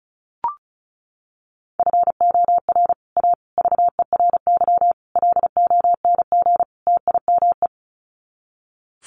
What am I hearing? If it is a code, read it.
Morse code, heard as a keyed tone, about 34 wpm, 700 Hz. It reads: FOR A VERY LONG TIME
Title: Sets of 5 Words 35wpm